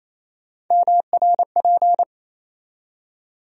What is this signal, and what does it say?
Morse code, heard as a keyed tone, about 28 wpm, 700 Hz. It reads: MRP
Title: USA Call Sign Postfixes - 3 Letters 28wpm